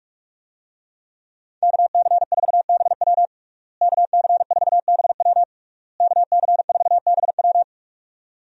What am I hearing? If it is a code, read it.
Morse code, heard as a keyed tone, about 46 wpm, 700 Hz. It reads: KC4BW KC4BW KC4BW
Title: USA Call Signs - 5 Characters - Repeated 3x - 45wpm